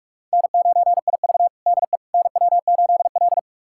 Morse code, heard as a keyed tone, about 41 wpm, 700 Hz. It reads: N9IV DE NW8L
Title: USA Callsign DE Callsign - 4 Characters - 45wpm